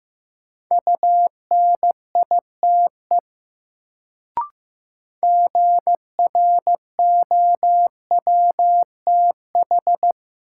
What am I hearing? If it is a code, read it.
Morse code, heard as a keyed tone, about 15 wpm, 700 Hz. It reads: UNITE GROWTH
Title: Sentences from Top 1000 Words - Review Part 5 - New Word Drill - 15wpm